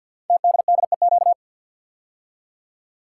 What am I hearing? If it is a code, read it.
Morse code, heard as a keyed tone, about 50 wpm, 700 Hz. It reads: TDDEQ
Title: Code Groups - 5 Letters - 50wpm